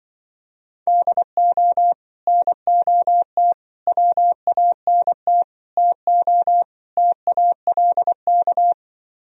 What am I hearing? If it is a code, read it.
Morse code, heard as a keyed tone, about 23 wpm, 700 Hz. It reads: DO NOT WANT TO TALK
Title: Speed-Racing - Sets of 5 Words 20wpm